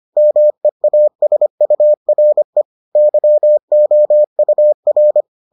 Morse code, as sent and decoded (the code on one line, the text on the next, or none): -- . .- ... ..- .-. / -.-- --- ..- .-.
MEASURE YOUR